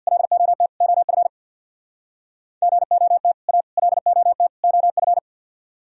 Morse code, heard as a keyed tone, about 66 wpm, 700 Hz. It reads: LOT OF GOT A LOT OF